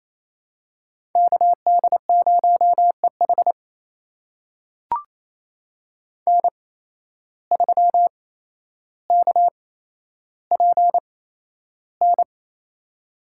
Morse code, heard as a keyed tone, about 28 wpm, spaced as for 12 wpm, 700 Hz. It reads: KD0EH N3KPN
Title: USA Call Signs - 5 Characters - 8x Character Spacing - 28wpm